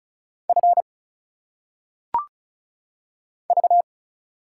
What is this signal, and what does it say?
Morse code, heard as a keyed tone, about 36 wpm, 700 Hz. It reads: F V
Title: Single Letters - Phonetic Pronunciation 35wpm